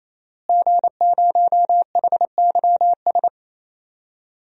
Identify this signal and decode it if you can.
Morse code, heard as a keyed tone, about 27 wpm, 700 Hz. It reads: G0HYS
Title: DXCC-100 Call Signs - 5 Characters - Repeated 3x - 28wpm